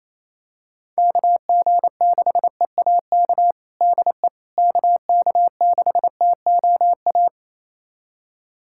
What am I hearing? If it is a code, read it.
Morse code, heard as a keyed tone, about 28 wpm, 700 Hz. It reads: KG6EAK DE KK6TOA
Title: USA Callsign DE Callsign - 6 Characters - 28wpm